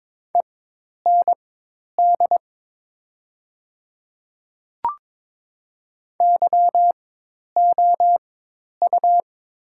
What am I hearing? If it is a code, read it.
Morse code, heard as a keyed tone, about 22 wpm, spaced as for 10 wpm, 700 Hz. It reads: END YOU
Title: 4x Character Spacing - Top 300 Words - 22wpm